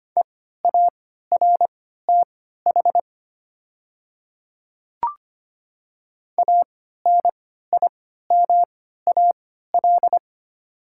Morse code, heard as a keyed tone, about 25 wpm, spaced as for 14 wpm, 700 Hz. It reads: EARTH ANIMAL
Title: 3x Character Spacing - Top 500 Words - 25wpm